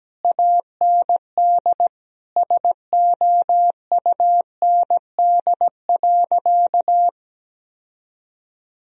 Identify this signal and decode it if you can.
Morse code, heard as a keyed tone, about 17 wpm, 700 Hz. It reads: AND SOUND.
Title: Sentences from Top 1000 Words - Review Part 1 - 17wpm